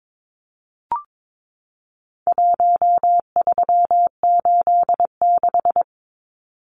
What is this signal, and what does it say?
Morse code, heard as a keyed tone, about 22 wpm, 700 Hz. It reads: 1386
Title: Numbers - Contest Style - 4-Digit 22wpm